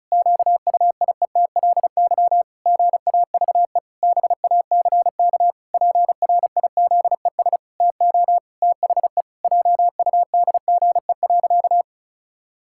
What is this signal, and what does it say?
Morse code, heard as a keyed tone, about 36 wpm, 700 Hz. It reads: QUIETLY GAVE BACK PRIZES TO THE JUDGE.
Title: Pangram Challenge v2 - 35wpm